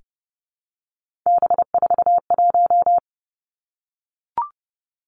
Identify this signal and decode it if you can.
Morse code, heard as a keyed tone, about 30 wpm, 700 Hz. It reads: B41
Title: Three Challenging Letter-Number Combinations 30wpm